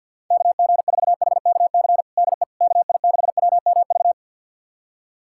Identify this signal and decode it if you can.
Morse code, heard as a keyed tone, about 51 wpm, 700 Hz. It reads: KG4HKC DE KI6WMV
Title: USA Callsign DE Callsign 50wpm